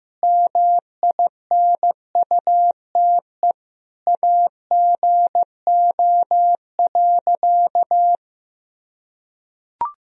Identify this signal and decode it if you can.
Morse code, heard as a keyed tone, about 15 wpm, 700 Hz. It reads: MINUTE AGO.
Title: Sentences from Top 500 Words and Abbreviations 15wpm